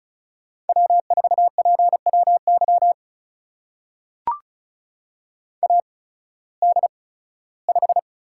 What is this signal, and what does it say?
Morse code, heard as a keyed tone, about 36 wpm, spaced as for 19 wpm, 700 Hz. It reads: W4PWY AD5
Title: USA Call Signs - 5 Characters - 8x Character Spacing - 35wpm